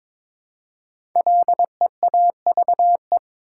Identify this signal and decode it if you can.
Morse code, heard as a keyed tone, about 22 wpm, 700 Hz. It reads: LEAVE
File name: 5 Letter Words - First 3 Letters Given - 22wpm